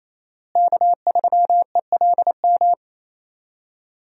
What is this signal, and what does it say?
Morse code, heard as a keyed tone, about 28 wpm, 700 Hz. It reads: K3ELM
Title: USA Call Signs - 5 Characters - Repeated 2x - 28wpm